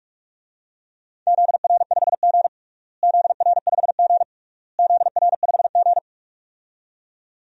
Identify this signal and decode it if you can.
Morse code, heard as a keyed tone, about 46 wpm, 700 Hz. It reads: ZR5G ZR5G ZR5G
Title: DXCC-100 Call Signs - 4 Characters - Repeated 3x - 45wpm